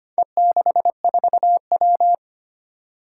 Morse code, heard as a keyed tone, about 25 wpm, 700 Hz. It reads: E64W
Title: Four Challenging Letter-Number Combinations 25wpm